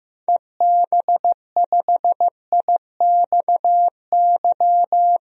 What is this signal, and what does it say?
Morse code, heard as a keyed tone, about 15 wpm, 700 Hz. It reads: EB5IXY
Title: DXCC-100 Call Signs - 6 Characters - Repeated 3x - 15wpm